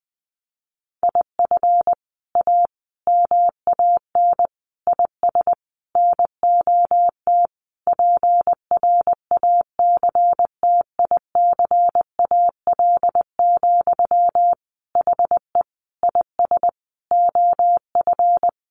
Morse code, heard as a keyed tone, about 20 wpm, 700 Hz. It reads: IF A MAN IS NOT PRACTICAL, HE IS OF
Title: Sentences from Top 1400 Words 20wpm